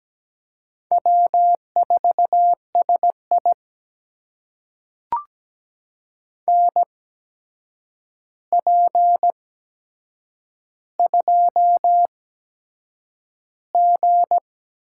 Morse code, heard as a keyed tone, about 17 wpm, spaced as for 7 wpm, 700 Hz. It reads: W4SI NP2G
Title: USA Call Signs - 4 Characters - 8x Character Spacing - 17wpm